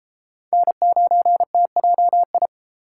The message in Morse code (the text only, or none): N9TJI